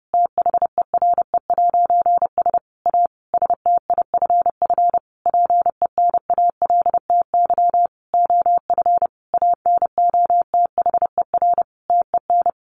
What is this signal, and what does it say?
Morse code, heard as a keyed tone, about 30 wpm, 700 Hz. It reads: THERE'S A STIFF PENALTY OF ANOTHER TEN
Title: A Sound Of Thunder 30wpm